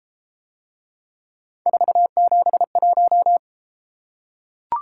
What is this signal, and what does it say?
Morse code, heard as a keyed tone, about 31 wpm, 700 Hz. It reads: 471